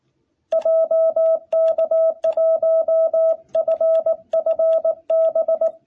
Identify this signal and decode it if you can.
Morse code, heard as a keyed tone, about 18 wpm, 660 Hz. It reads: JK1FFB
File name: jh8cbh / 2021cw